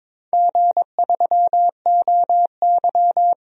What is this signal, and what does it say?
Morse code, heard as a keyed tone, about 22 wpm, 700 Hz. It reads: G3OY
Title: DXCC-100 Call Signs - 4 Characters - 22wpm